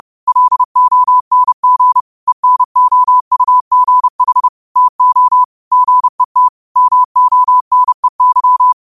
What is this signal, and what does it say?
Morse code, heard as a keyed tone, about 30 wpm, 1000 Hz. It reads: LONG ENOUGH TO GET MONEY
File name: qotdfiles / 30wpm